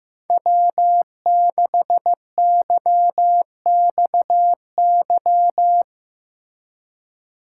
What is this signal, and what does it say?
Morse code, heard as a keyed tone, about 15 wpm, 700 Hz. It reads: W6YXY
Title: USA Call Signs - 5 Characters - 3x Character Spacing - 15wpm